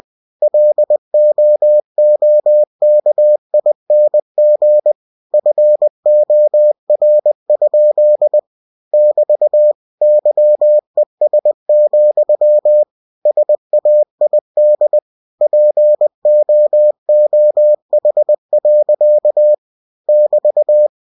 Morse code, heard as a keyed tone, about 20 wpm, 600 Hz. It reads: LOOKING FOR? <BT> YES, SAID POOH. <BT>